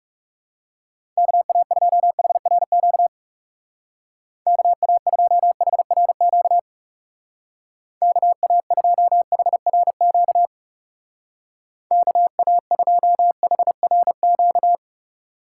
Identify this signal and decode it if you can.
Morse code, heard as a keyed tone, about 36 wpm, 700 Hz. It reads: KA2HRQ KA2HRQ KA2HRQ KA2HRQ
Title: Speed Racing - USA 2x3 Call Signs 30wpm